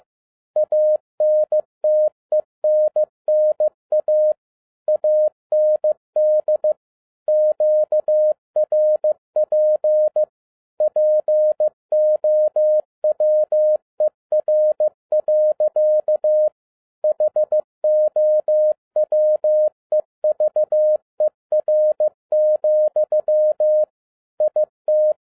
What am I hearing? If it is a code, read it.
Morse code, heard as a keyed tone, about 15 wpm, 610 Hz. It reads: ANTENNA AND QRP POWER. HOWEVER, IT
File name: 2011-07-mp3 / n42 EXCITING QRP CW OPERATION 15wpm